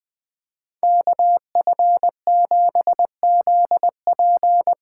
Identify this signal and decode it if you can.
Morse code, heard as a keyed tone, about 20 wpm, 700 Hz. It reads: KF7ZP
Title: USA Call Signs - 6 Characters - 6x Character Spacing - 20wpm